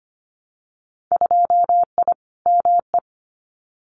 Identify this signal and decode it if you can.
Morse code, heard as a keyed tone, about 25 wpm, 700 Hz. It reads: 2I ME
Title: Winter Field Day Exchanges Encore 25wpm